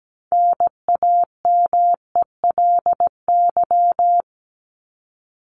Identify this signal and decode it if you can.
Morse code, heard as a keyed tone, about 17 wpm, 700 Hz. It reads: NAMELY
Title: LY Suffix - Top 100 Words with Base Words 17wpm